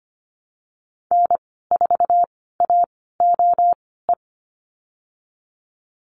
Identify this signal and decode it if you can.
Morse code, heard as a keyed tone, about 25 wpm, spaced as for 16 wpm, 700 Hz. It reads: N4AOE
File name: Lesson 15g - Number 5 - Call Signs 25wpm